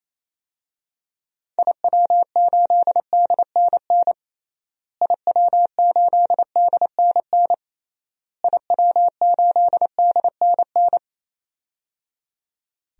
Morse code, heard as a keyed tone, about 28 wpm, 700 Hz. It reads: IW8DNN IW8DNN IW8DNN
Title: DXCC-100 Call Signs - 6 Characters - Repeated 3x - 28wpm